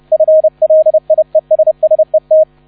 Morse code, heard as a keyed tone, about 30 wpm, 630 Hz, with radio noise. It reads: FLIESSET